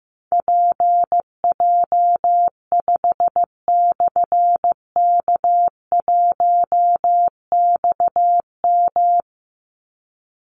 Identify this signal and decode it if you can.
Morse code, heard as a keyed tone, about 14 wpm, 700 Hz. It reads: PJ5/K1XM
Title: Call Signs Repeated 3x - 15wpm